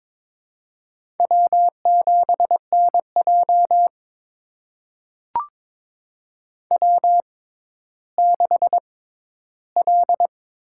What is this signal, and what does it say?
Morse code, heard as a keyed tone, about 22 wpm, spaced as for 13 wpm, 700 Hz. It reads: W7NJ W6L
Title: USA Call Signs - 6x Character Spacing - 22wpm